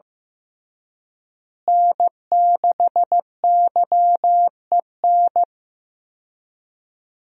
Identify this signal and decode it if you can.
Morse code, heard as a keyed tone, about 15 wpm, 700 Hz. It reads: N6YEN
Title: USA Call Signs - 5 Characters - Repeated 3x - 15wpm